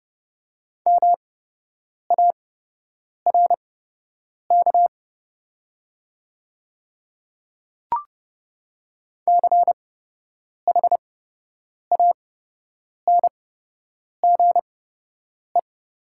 Morse code, heard as a keyed tone, about 30 wpm, spaced as for 8 wpm, 700 Hz. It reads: MARK CHANGE
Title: 8x Character Spacing - Top 300 Words - 30wpm